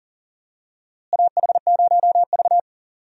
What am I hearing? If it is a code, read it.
Morse code, heard as a keyed tone, about 40 wpm, 700 Hz. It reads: AH0V
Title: USA Call Signs - 4 Characters - Repeated 3x - 40wpm